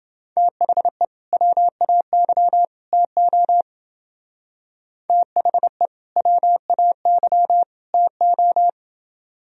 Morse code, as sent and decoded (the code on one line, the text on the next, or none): - .... / .-- .- -.-- / - --- / - .... / .-- .- -.-- / - ---
THE WAY TO THE WAY TO